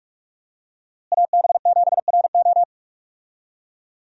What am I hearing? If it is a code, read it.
Morse code, heard as a keyed tone, about 46 wpm, 700 Hz. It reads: AB7RO
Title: USA Call Signs - 5 Characters - Repeated 3x - 45wpm